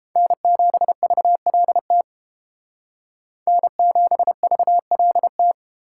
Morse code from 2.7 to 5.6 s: N7VLT

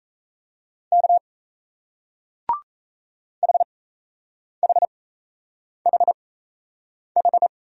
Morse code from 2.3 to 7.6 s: HHHH

